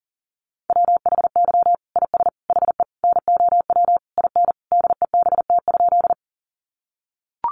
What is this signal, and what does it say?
Morse code, heard as a keyed tone, about 40 wpm, 700 Hz. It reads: WHY IS HE NOW IN DEBT?
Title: Sentences from Top 1300 Words 40wpm